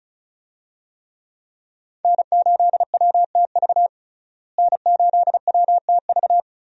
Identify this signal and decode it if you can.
Morse code, heard as a keyed tone, about 34 wpm, 700 Hz. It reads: N8WTV N8WTV
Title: USA Call Signs - 5 Characters - Repeated 2x - 35wpm